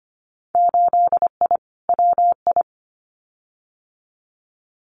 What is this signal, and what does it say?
Morse code, heard as a keyed tone, about 25 wpm, 690 Hz. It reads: 8I WI
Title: Winter Field Day Exchanges 25wpm